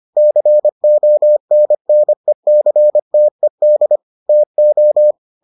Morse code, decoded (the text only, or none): CONNECTED TO